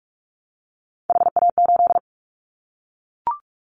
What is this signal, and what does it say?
Morse code, heard as a keyed tone, about 44 wpm, 700 Hz. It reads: HA8
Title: Callsign Prefixes for Top 25 DXCC - Repeated 3x - 45wpm